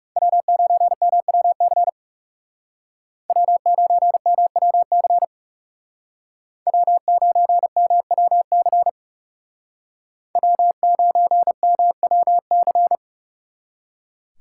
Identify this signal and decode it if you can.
Morse code, heard as a keyed tone, about 40 wpm, 700 Hz. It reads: W9MWC W9MWC W9MWC W9MWC